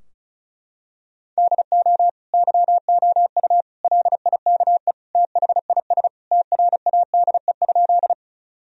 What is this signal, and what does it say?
Morse code, heard as a keyed tone, about 34 wpm, 700 Hz. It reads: DO YOU LIKE THIS TRADE?
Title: Sentences from Top 1000 Words - Review Part 4 - 35wpm